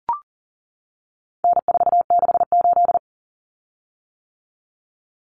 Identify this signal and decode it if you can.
Morse code, heard as a keyed tone, about 40 wpm, 700 Hz. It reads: N468
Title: Common Cut Numbers - 4-Digit 40wpm